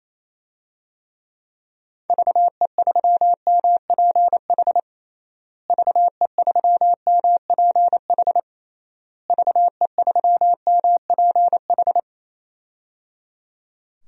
Morse code, heard as a keyed tone, about 28 wpm, 700 Hz. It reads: VE3MPH VE3MPH VE3MPH
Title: DXCC-100 Call Signs - 6 Characters - Repeated 3x - 28wpm